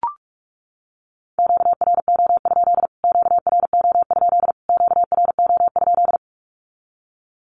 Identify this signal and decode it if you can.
Morse code, heard as a keyed tone, about 44 wpm, 700 Hz. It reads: QRO? QRO? QRO?
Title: Q-Codes Repeated 3x 45wpm